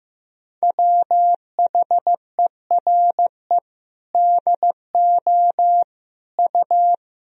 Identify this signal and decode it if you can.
Morse code, heard as a keyed tone, about 16 wpm, 700 Hz. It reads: WHERE DO U
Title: Sentences from Top 200 Words and Abbreviations 15wpm